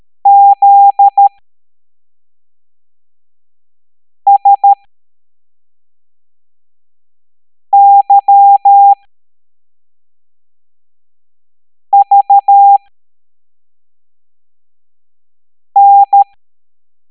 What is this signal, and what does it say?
Morse code, heard as a keyed tone, about 13 wpm, spaced as for 3 wpm, 800 Hz. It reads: ZSYVN
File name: hamdata / L05-EX01-SVY